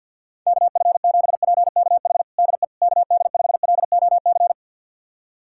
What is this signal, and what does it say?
Morse code, heard as a keyed tone, about 50 wpm, 700 Hz. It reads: KF7PKH DE KD5LOC
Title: USA Callsign DE Callsign 50wpm